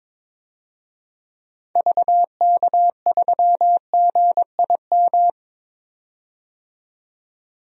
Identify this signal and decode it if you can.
Morse code, heard as a keyed tone, about 21 wpm, 700 Hz. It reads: VK3GIM
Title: DXCC-100 Call Signs - 6 Characters - 22wpm